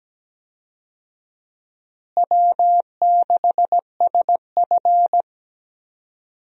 Morse code, heard as a keyed tone, about 17 wpm, 700 Hz. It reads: W6SF
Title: USA Call Signs - 4 Characters - Repeated 3x - 17wpm